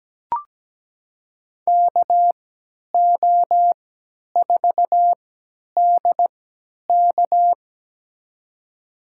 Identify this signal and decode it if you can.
Morse code, heard as a keyed tone, about 17 wpm, spaced as for 10 wpm, 700 Hz. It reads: KO4DK
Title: USA Call Signs - 5 Characters - 3x Character Spacing - 17wpm